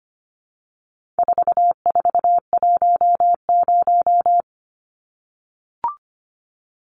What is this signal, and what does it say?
Morse code, heard as a keyed tone, about 25 wpm, 700 Hz. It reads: 4410
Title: Numbers - 4-Digit 25wpm